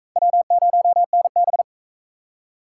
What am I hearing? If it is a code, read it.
Morse code, heard as a keyed tone, about 35 wpm, 700 Hz. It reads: W0NB
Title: Speed Racing - USA 1x2 Call Signs 28wpm